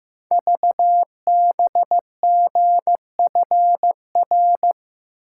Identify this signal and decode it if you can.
Morse code, heard as a keyed tone, about 15 wpm, 700 Hz. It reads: VBGFR
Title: Code Groups - 5 Letters - 15wpm